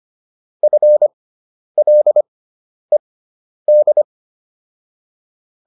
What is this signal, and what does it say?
Morse code, heard as a keyed tone, about 25 wpm, spaced as for 10 wpm, 600 Hz. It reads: FLED